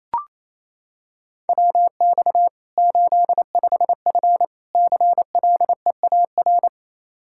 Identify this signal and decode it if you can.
Morse code, heard as a keyed tone, about 28 wpm, 700 Hz. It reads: WX 85F CLEAR
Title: Practice QSO - Exchange 2 - Info - 28wpm